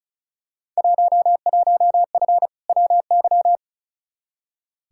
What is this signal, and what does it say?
Morse code, heard as a keyed tone, about 36 wpm, 700 Hz. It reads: 11F WY
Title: Field Day Exchanges Encore - 35wpm